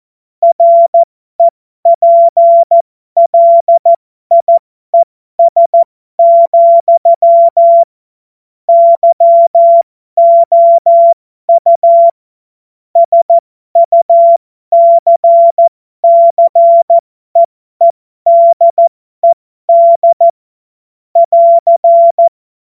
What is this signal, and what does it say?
Morse code, heard as a keyed tone, about 14 wpm, spaced as for 12 wpm, 680 Hz. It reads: REPLIES, YOU SUCCEEDED <AR>
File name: Morsekurs / Witze